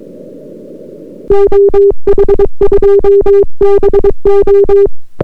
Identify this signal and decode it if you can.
Morse code, heard as a keyed tone, about 22 wpm, 400 Hz. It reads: OH2BO